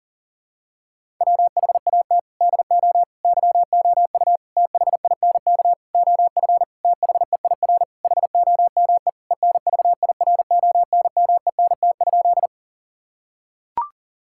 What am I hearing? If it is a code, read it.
Morse code, heard as a keyed tone, about 37 wpm, 700 Hz. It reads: WHAT DO YOU THINK OF THEIR HOME ENVIRONMENT?
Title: Sentences from Top 1000 Words - Review Part 5 - 40wpm